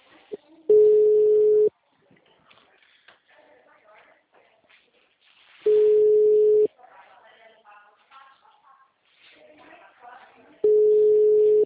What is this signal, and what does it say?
Ringback tone (Europe, 425 Hz), with speech.